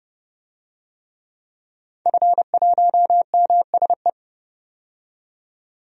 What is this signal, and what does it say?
Morse code, heard as a keyed tone, about 30 wpm, 700 Hz. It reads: F1MSE
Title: DXCC-100 Call Signs - 5 Characters - 30wpm